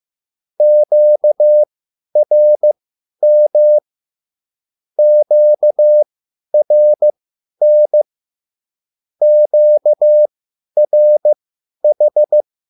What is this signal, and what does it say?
Morse code, heard as a keyed tone, about 15 wpm, spaced as for 10 wpm, 600 Hz. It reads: QRM QRN QRH
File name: QSL / CW lesson